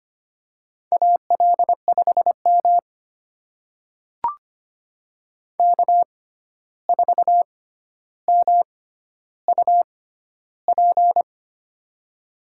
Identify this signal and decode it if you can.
Morse code, heard as a keyed tone, about 25 wpm, spaced as for 12 wpm, 700 Hz. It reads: AL5M K4MUP
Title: USA Call Signs - 6x Character Spacing - 25wpm